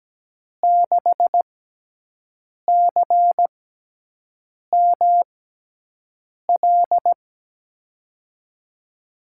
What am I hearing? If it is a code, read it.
Morse code, heard as a keyed tone, about 17 wpm, spaced as for 6 wpm, 700 Hz. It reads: 6CML
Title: USA Call Signs - 5 Characters - 6x Character Spacing - 17wpm